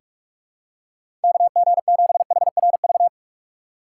Morse code, heard as a keyed tone, about 44 wpm, 700 Hz. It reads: KG7HRV
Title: USA Call Signs - 4x Character Spacing - 45wpm